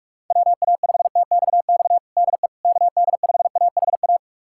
Morse code, heard as a keyed tone, about 46 wpm, 700 Hz. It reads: WA5TXX DE KD5AHA
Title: USA Callsign DE Callsign - 6 Characters - 45wpm